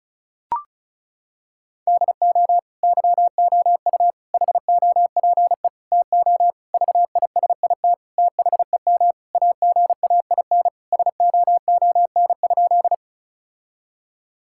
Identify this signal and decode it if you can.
Morse code, heard as a keyed tone, about 36 wpm, 700 Hz. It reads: DO YOU HOPE TO VISIT THEM AGAIN SOON?
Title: Sentences from Top 1000 Words - Review Part 6 - 35wpm